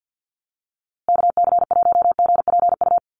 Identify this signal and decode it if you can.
Morse code, heard as a keyed tone, about 50 wpm, 700 Hz. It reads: KC1GPU
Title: USA Call Signs Repeated 2x 50wpm